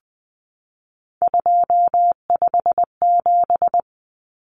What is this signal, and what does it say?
Morse code, heard as a keyed tone, about 20 wpm, 700 Hz. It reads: 257
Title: Common Cut Numbers - 3-Digit 20wpm